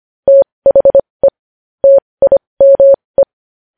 Morse code, heard as a keyed tone, about 25 wpm, spaced as for 20 wpm, 550 Hz. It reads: THE TIME
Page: news headlines in morse code at 20 wpm